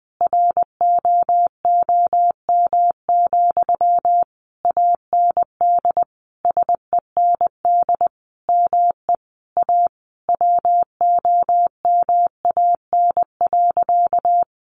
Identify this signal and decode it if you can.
Morse code, heard as a keyed tone, about 20 wpm, 700 Hz. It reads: ROOM, AND SEND ME A WOMAN.
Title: Sentences from Top 1500 Words 20wpm